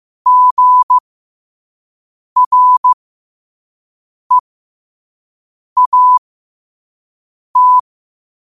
Morse code, heard as a keyed tone, about 15 wpm, spaced as for 5 wpm, 1000 Hz. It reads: GREAT
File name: qotdfiles / 5wpm